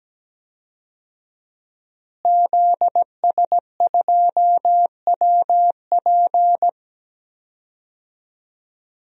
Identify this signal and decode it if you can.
Morse code, heard as a keyed tone, about 17 wpm, 700 Hz. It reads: ZS2WP
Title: DXCC-100 Call Signs - 5 Characters - Repeated 3x - 17wpm